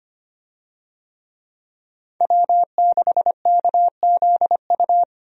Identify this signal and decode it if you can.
Morse code, heard as a keyed tone, about 25 wpm, 700 Hz. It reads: W6KZU
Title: USA Call Signs - 5 Characters - 25wpm